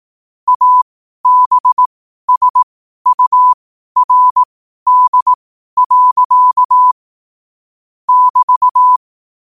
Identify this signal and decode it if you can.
Morse code, heard as a keyed tone, about 18 wpm, spaced as for 12 wpm, 1000 Hz. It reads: ABSURD. <BT>
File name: qotdfiles / 12wpm